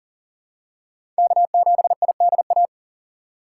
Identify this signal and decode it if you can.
Morse code, heard as a keyed tone, about 40 wpm, 700 Hz. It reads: K7IDA
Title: USA Call Signs - 5 Characters - 2x Character Spacing - 40wpm